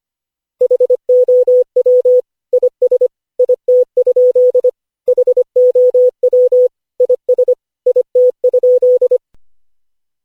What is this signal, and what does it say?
Morse code, heard as a keyed tone, about 25 wpm, 500 Hz. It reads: HOW IS IT? HOW IS IT?
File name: cw / Session5 cw training